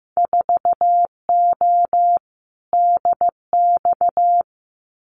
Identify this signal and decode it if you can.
Morse code, heard as a keyed tone, about 15 wpm, 700 Hz. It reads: 4O DX
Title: Winter Field Day Exchanges Encore 15wpm